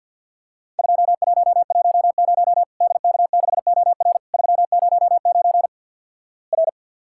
Morse code, heard as a keyed tone, about 50 wpm, 700 Hz. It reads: 2110 DK6OR 309 R
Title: Worked All Europe DX Contest - QTC Exchange - 50wpm